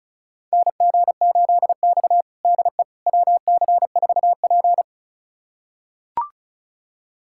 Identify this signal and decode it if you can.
Morse code, heard as a keyed tone, about 36 wpm, 700 Hz. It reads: NG8X DE WC4P
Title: USA Callsign DE Callsign - 4 Characters - 35wpm